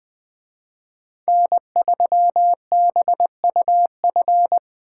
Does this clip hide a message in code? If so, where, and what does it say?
Yes: Morse code from 1.3 to 4.6 s, heard as a keyed tone, about 20 wpm, 700 Hz. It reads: N3BUF